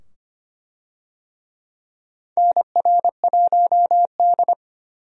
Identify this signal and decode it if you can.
Morse code, heard as a keyed tone, about 25 wpm, 700 Hz. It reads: NR1D
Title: USA Call Signs - 4 Characters - 25wpm